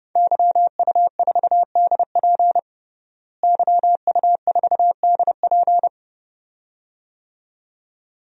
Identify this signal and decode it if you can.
Morse code, heard as a keyed tone, about 32 wpm, 700 Hz. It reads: YU4DP YU4DP
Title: DXCC-100 Call Signs - 5 Characters - Repeated 2x - 30wpm